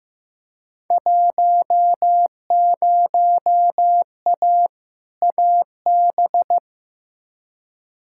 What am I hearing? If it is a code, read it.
Morse code, heard as a keyed tone, about 15 wpm, 700 Hz. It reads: 10A AB
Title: Field Day Exchanges Encore - 15wpm